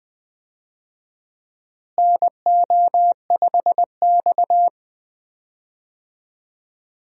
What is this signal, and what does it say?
Morse code, heard as a keyed tone, about 21 wpm, 700 Hz. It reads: NO5X
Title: USA Call Signs - 4 Characters - Repeated 2x - 20wpm